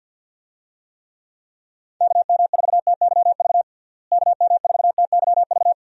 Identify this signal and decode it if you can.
Morse code, heard as a keyed tone, about 52 wpm, 700 Hz. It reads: KM4TYV KM4TYV
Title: USA Call Signs - 6 Characters - Repeated 3x - 50wpm